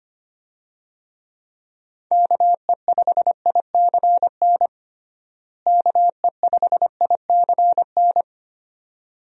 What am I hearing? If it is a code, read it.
Morse code, heard as a keyed tone, about 25 wpm, 700 Hz. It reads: KE5ICN KE5ICN
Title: USA Call Signs - 6 Characters - Repeated 2x - 25wpm